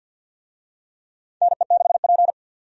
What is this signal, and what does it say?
Morse code, heard as a keyed tone, about 47 wpm, 700 Hz. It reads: NE6P
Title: USA Call Signs - 4 Characters - 50wpm